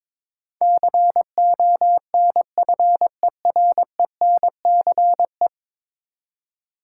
Morse code, heard as a keyed tone, about 22 wpm, 700 Hz. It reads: CONFERENCE
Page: morse-code-ninja-practice-22wpm